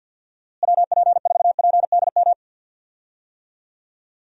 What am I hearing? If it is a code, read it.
Morse code, heard as a keyed tone, about 52 wpm, 700 Hz. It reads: WP4PDM
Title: USA Call Signs - 6 Characters - Repeated 3x - 50wpm